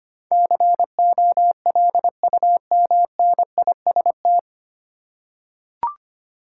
Morse code, heard as a keyed tone, about 25 wpm, 700 Hz. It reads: COLUMNIST
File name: Nine Letter Words - 25wpm